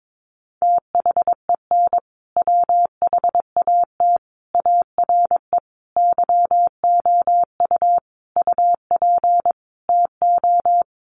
Morse code, heard as a keyed tone, about 22 wpm, 700 Hz. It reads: THEN WHAT ARE YOU UP TO
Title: Sentences from Top 100 Words 22wpm